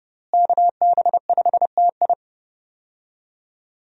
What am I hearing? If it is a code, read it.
Morse code, heard as a keyed tone, about 30 wpm, 700 Hz. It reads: KB5TI
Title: USA Call Signs - 5 Characters - Repeated 2x - 30wpm